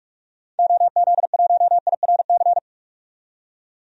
Morse code, heard as a keyed tone, about 43 wpm, 700 Hz. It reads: OZ1IRC
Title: DXCC-100 Call Signs - 6 Characters - 45wpm